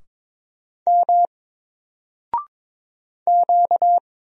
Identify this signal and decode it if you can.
Morse code, heard as a keyed tone, about 22 wpm, 700 Hz. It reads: M Q
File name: Single-Letter-Number-22wpm